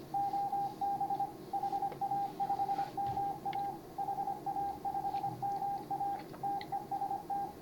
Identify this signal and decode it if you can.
Morse code, heard as a keyed tone, about 25 wpm, 800 Hz, with radio noise. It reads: OB HA5MA HA5MA TEST
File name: CW / HA5MA RHOB HANG CW 2017